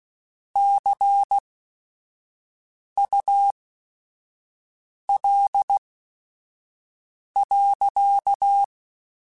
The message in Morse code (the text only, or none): CUL.